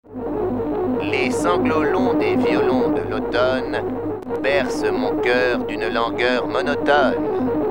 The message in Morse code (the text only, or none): none